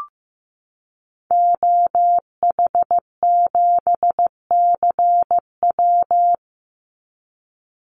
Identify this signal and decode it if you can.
Morse code, heard as a keyed tone, about 15 wpm, 700 Hz. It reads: OH7CW
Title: Call Signs Repeated 2x - 15wpm